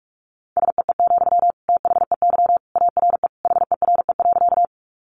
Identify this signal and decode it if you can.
Morse code, heard as a keyed tone, about 44 wpm, 700 Hz. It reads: SEE, THEY ARE HERE.
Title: Sentences from Top 200 Words 45wpm